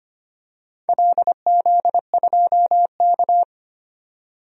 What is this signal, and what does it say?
Morse code, heard as a keyed tone, about 25 wpm, 700 Hz. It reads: LZ2K
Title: DXCC-100 Call Signs - 4 Characters - 25wpm